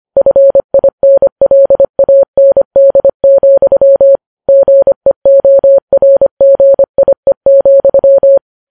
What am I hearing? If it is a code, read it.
Morse code, heard as a keyed tone, about 25 wpm, 550 Hz. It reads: FINLAND, GEORGIE,